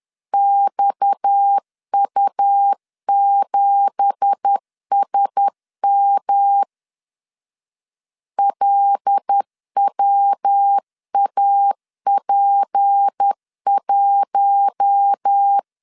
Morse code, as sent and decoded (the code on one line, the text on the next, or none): -..- ..- --... ... -- / .-.. .-- .- .--. .----
XU7SM LWAP1